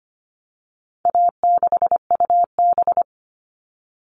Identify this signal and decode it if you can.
Morse code, heard as a keyed tone, about 25 wpm, 700 Hz. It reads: A6UB
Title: All Cut Numbers - 4-Digit 25wpm